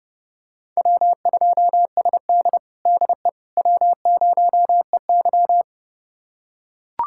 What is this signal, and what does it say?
Morse code, heard as a keyed tone, about 30 wpm, 700 Hz. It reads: W2SD DE W0EY
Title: USA Callsign DE Callsign - 4 Characters - 30wpm